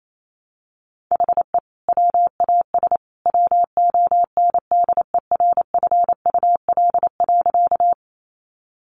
Morse code, heard as a keyed tone, about 28 wpm, 700 Hz. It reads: HE WAS WONDERFUL.
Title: Sentences from Top 1100 Words 28wpm